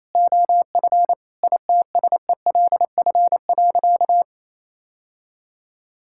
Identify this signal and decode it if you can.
Morse code, heard as a keyed tone, about 28 wpm, 700 Hz. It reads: OF ITSELF.
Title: Sentences from Top 1000 Words - Review Part 5 - 28wpm